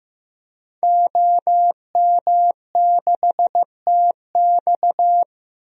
Morse code, heard as a keyed tone, about 15 wpm, 700 Hz. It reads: OM6TX
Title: DXCC-100 Call Signs - 5 Characters - 15wpm